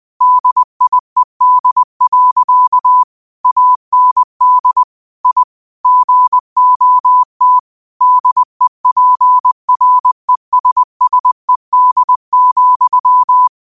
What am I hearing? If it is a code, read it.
Morse code, heard as a keyed tone, about 20 wpm, 1000 Hz. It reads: DIED. AND I GOT DEPRESSED,
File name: qotdfiles / 20wpm